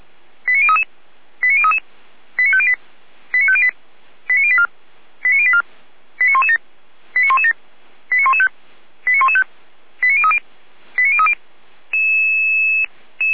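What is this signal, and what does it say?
Five-tone selective call (ZVEI), heard as quick runs of five tones, tones of about 70 ms. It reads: 89033 89033 89508 89508 89084 89084 89117 89117 89115 89115 89030 89030
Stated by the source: ZVEI five-tone call received by SDR.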